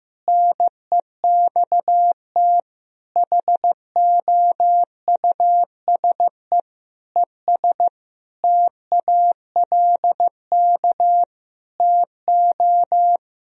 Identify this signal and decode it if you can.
Morse code, heard as a keyed tone, about 15 wpm, 700 Hz. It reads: NEXT HOUSE ES TALK TO
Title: Sentences from Top 400 Words and Abbreviations 15wpm